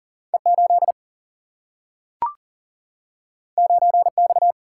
Morse code, heard as a keyed tone, about 40 wpm, 700 Hz. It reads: E 8 9 X
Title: Single-Letter Single-Number 40wpm